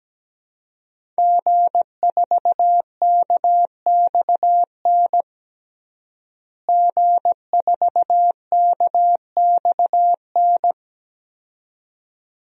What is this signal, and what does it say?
Morse code, heard as a keyed tone, about 17 wpm, 700 Hz. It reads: G4KXN G4KXN
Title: DXCC-100 Call Signs - 5 Characters - Repeated 2x - 17wpm